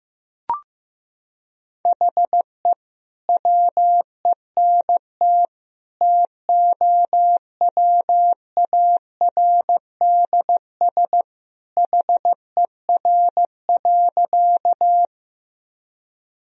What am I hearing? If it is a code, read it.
Morse code, heard as a keyed tone, about 15 wpm, 700 Hz. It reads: HE WENT TOWARDS HER.
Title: Sentences from Top 1000 Words - Review Part 2 - 15wpm